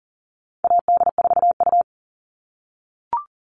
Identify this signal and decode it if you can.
Morse code, heard as a keyed tone, about 40 wpm, 700 Hz. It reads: AD4U
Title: All Cut Numbers - 4-Digit 40wpm